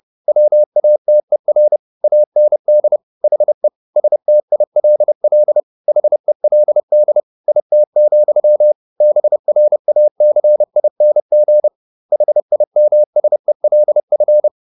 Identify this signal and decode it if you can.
Morse code, heard as a keyed tone, about 30 wpm, 600 Hz. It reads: WATER AND HE STILL HELD IT, BRACING HIMSELF
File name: the old man and the sea / the old man and the sea 1 30wpm